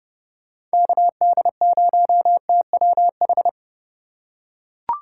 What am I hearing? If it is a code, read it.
Morse code, heard as a keyed tone, about 29 wpm, 700 Hz. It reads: KD0TWH